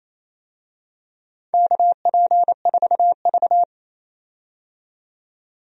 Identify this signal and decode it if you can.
Morse code, heard as a keyed tone, about 28 wpm, 700 Hz. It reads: KP4V
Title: DXCC-100 Call Signs - 4 Characters - 28wpm